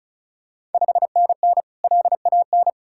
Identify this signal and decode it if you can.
Morse code, heard as a keyed tone, about 36 wpm, 700 Hz. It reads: 5NN LAN
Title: Ontario QSO Party Exchange 35wpm